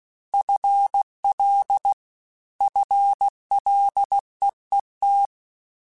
Morse code, heard as a keyed tone, about 16 wpm, 800 Hz. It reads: FL FLEET
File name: Combinations / Combinations-mp3s / Combinations@15